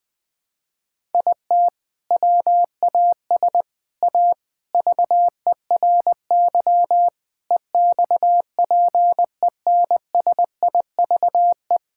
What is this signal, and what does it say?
Morse code, heard as a keyed tone, about 20 wpm, 700 Hz. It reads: IT WAS A VERY EXPENSIVE
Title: Sentences from Top 1000 Words - Review Part 6 - 20wpm